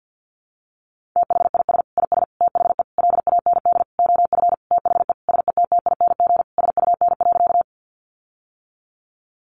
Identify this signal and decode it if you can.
Morse code, heard as a keyed tone, about 50 wpm, 700 Hz. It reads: THIS IS THE LAND OF THE SETTING SUN.
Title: Sentences from Top 1300 Words 50wpm